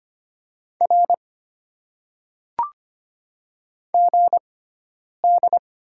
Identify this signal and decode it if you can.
Morse code, heard as a keyed tone, about 25 wpm, spaced as for 9 wpm, 690 Hz. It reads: R GD
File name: Top 90 QSO Elements - 6x Character Spacing - 25wpm